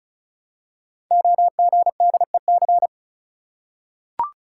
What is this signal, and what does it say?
Morse code, heard as a keyed tone, about 36 wpm, 700 Hz. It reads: OGDEC
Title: Code Groups - 5 Letters - 35wpm